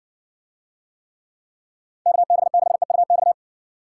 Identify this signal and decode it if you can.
Morse code, heard as a keyed tone, about 60 wpm, 700 Hz. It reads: KB6EUX